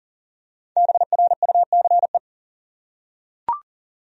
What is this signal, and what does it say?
Morse code, heard as a keyed tone, about 40 wpm, 700 Hz. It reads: BRUCE
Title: USA Common Names - Max 5 Letters - 40wpm